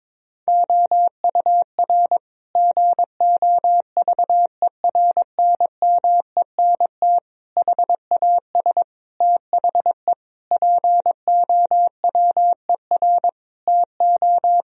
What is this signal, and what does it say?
Morse code, heard as a keyed tone, about 21 wpm, 700 Hz. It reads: OUR GOVERNMENT HAS THE POWER TO
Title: Sentences from Top 1000 Words - Review Part 3 - 22wpm